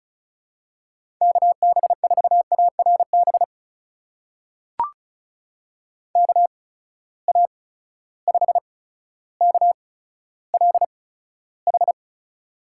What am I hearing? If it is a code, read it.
Morse code, heard as a keyed tone, about 36 wpm, spaced as for 15 wpm, 700 Hz. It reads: KB4ARB KA5KLH
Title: USA Call Signs - 6 Characters - 8x Character Spacing - 35wpm